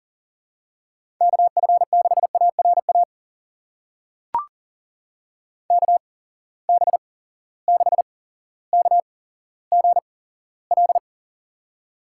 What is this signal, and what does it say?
Morse code, heard as a keyed tone, about 40 wpm, spaced as for 17 wpm, 700 Hz. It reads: KF6ARA KB6KGL
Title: USA Call Signs - 6 Characters - 8x Character Spacing - 40wpm